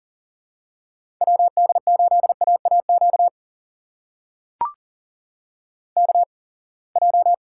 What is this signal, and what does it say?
Morse code, heard as a keyed tone, about 40 wpm, spaced as for 28 wpm, 700 Hz. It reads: WD8AAQ KJ
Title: USA Call Signs - 6 Characters - 8x Character Spacing - 40wpm